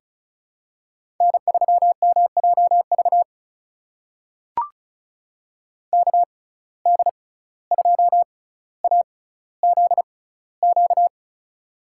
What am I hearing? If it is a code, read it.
Morse code, heard as a keyed tone, about 36 wpm, spaced as for 17 wpm, 700 Hz. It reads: N3MJV KD2AZQ
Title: USA Call Signs - 6x Character Spacing - 35wpm